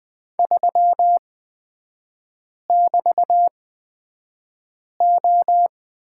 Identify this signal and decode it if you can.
Morse code, heard as a keyed tone, about 20 wpm, spaced as for 5 wpm, 700 Hz. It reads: 3<BT>O
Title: 250 Lesson 40e - BK - Three Characters 20wpm